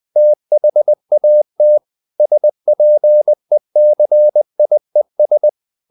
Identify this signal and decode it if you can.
Morse code, heard as a keyed tone, about 20 wpm, 600 Hz. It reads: THAT SPECIES
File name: the old man and the sea / the old man and the sea part 1 20wpm